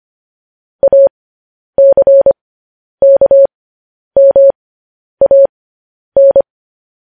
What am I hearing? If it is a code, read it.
Morse code, heard as a keyed tone, about 25 wpm, spaced as for 10 wpm, 550 Hz. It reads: ACKMAN
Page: news headlines in morse code at 10 wpm